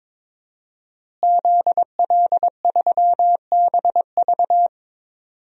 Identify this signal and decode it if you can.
Morse code, heard as a keyed tone, about 22 wpm, 700 Hz. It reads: ZL3BV
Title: DXCC-100 Call Signs - 5 Characters - Repeated 3x - 22wpm